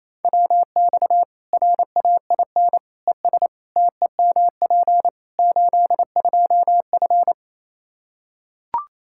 Morse code, heard as a keyed tone, about 28 wpm, 700 Hz. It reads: WX RAIN ES TEMP 82F
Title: Practice QSO - Exchange 2 - Info - 28wpm